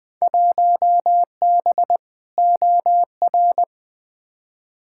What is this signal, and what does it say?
Morse code, heard as a keyed tone, about 20 wpm, 700 Hz. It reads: 1B OR
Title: Field Day Exchanges Encore - 20wpm